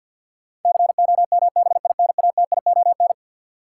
Morse code, heard as a keyed tone, about 53 wpm, 700 Hz. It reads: COMBINATION
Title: Sentences from Top 1000 Words - Review Part 6 - New Word Drill - 50wpm